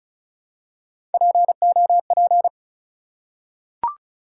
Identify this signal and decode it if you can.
Morse code, heard as a keyed tone, about 36 wpm, 700 Hz. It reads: POP
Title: Three Letter Words - 35wpm